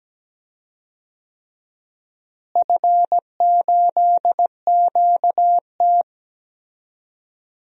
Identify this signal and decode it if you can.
Morse code, heard as a keyed tone, about 17 wpm, 700 Hz. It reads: F8QT
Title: DXCC-100 Call Signs - 4 Characters - Repeated 3x - 17wpm